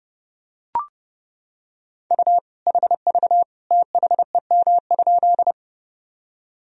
Morse code, heard as a keyed tone, about 30 wpm, 700 Hz. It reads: U HV THEM?
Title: Sentences from Top 100 Words and Abbreviations 30wpm